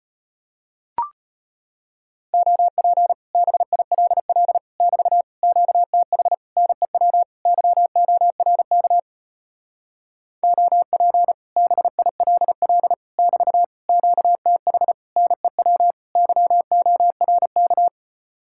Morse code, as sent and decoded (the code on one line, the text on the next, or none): --- .--. / -... .. .-.. .-.. / -...- / --.- - .... / -. .-- / -.-- --- .-. -.- / --- .--. / -... .. .-.. .-.. / -...- / --.- - .... / -. .-- / -.-- --- .-. -.-
OP BILL <BT> QTH NEW YORK OP BILL <BT> QTH NEW YORK